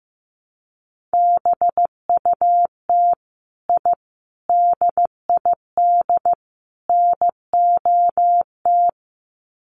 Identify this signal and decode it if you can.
Morse code, heard as a keyed tone, about 15 wpm, 700 Hz. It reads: BUT I DID NOT
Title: Sets of 4 Words 15wpm